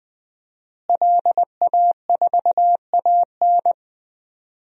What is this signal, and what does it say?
Morse code, heard as a keyed tone, about 20 wpm, 700 Hz. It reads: LA4AN
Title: DXCC-100 Call Signs - 5 Characters - Repeated 2x - 20wpm